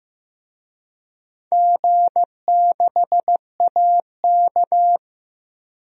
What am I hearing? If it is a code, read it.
Morse code, heard as a keyed tone, about 15 wpm, 700 Hz. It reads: G6AK
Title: DXCC-100 Call Signs - 4 Characters - Repeated 2x - 15wpm